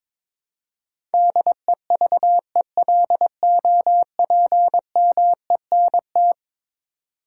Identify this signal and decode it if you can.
Morse code, heard as a keyed tone, about 22 wpm, 700 Hz. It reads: DEVELOPMENT
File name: practice / 22wpm / Eleven Letter Words - 22wpm